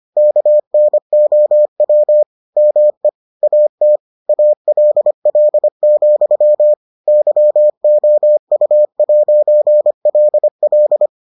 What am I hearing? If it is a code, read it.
Morse code, heard as a keyed tone, about 25 wpm, 600 Hz. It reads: KNOW ME AT ALL, YOU'LL